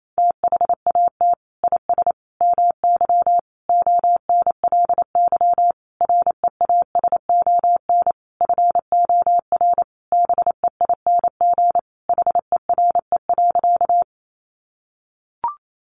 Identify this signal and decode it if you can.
Morse code, heard as a keyed tone, about 28 wpm, 700 Hz. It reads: THAT IS MY ONLY REASON FOR BEING HERE.
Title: Sentences from Top 700 Words 28wpm